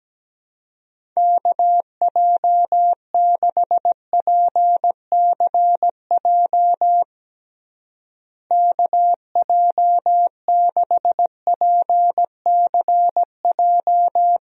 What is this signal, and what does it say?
Morse code, heard as a keyed tone, about 17 wpm, 700 Hz. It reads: KJ6PCJ KJ6PCJ
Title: USA Call Signs - 6 Characters - Repeated 2x - 17wpm